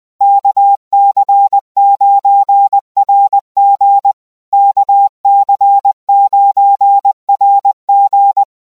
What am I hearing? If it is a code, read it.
Morse code, heard as a keyed tone, about 20 wpm, 800 Hz. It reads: KC9RG KC9RG
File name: cq-20wpm-800hz